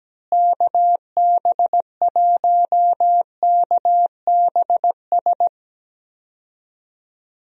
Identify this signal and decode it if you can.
Morse code, heard as a keyed tone, about 17 wpm, 700 Hz. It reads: KB1KBS
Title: USA Call Signs - 6 Characters - Repeated 3x - 17wpm